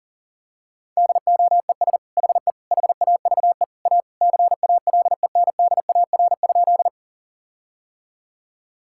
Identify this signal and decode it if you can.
Morse code, heard as a keyed tone, about 40 wpm, 700 Hz. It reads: DOES HE HAVE A CALENDAR?
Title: Sentences from Top 1600 Words 40wpm